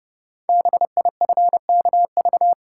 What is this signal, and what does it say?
Morse code, heard as a keyed tone, about 30 wpm, 700 Hz. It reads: BIFKV
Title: Code Groups - 5 Letter:Numbers - 30wpm